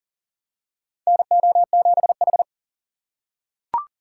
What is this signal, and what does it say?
Morse code, heard as a keyed tone, about 39 wpm, 700 Hz. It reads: NO7H
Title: USA Call Signs - 2x Character Spacing - 40wpm